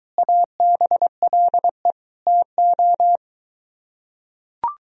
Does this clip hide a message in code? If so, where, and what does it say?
Yes: Morse code from 0.2 to 3.2 s, heard as a keyed tone, about 23 wpm, 700 Hz. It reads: ABLE TO